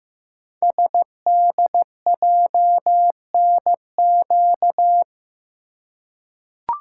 Morse code, heard as a keyed tone, about 15 wpm, 700 Hz. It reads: SDJNQ
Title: Code Groups - 5 Letter:Numbers - 15wpm